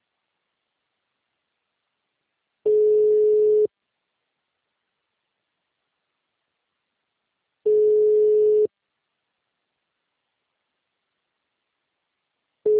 Ringback tone (Europe, 425 Hz).